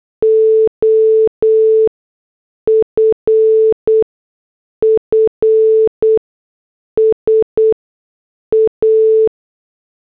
Morse code, heard as a keyed tone, about 8 wpm, spaced as for 6 wpm, 440 Hz. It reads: OFFS A